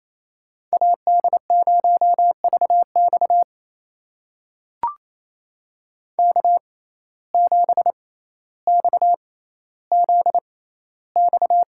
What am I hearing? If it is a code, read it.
Morse code, heard as a keyed tone, about 28 wpm, spaced as for 14 wpm, 700 Hz. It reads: AD0VX K7XZX
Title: USA Call Signs - 5 Characters - 6x Character Spacing - 28wpm